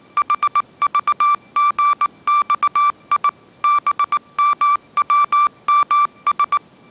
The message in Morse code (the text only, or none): HVGXI BMWMS